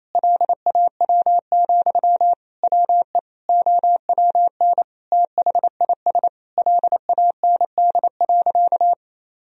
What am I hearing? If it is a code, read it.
Morse code, heard as a keyed tone, about 28 wpm, 700 Hz. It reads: LAW, WE OWN THIS LAND.
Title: Sentences from Top 1000 Words - Review Part 4 - 28wpm